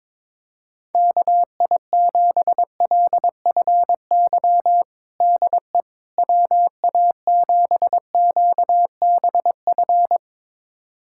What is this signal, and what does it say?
Morse code, heard as a keyed tone, about 22 wpm, 700 Hz. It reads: KI7LFY DE WA7QBF
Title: USA Callsign DE Callsign 22wpm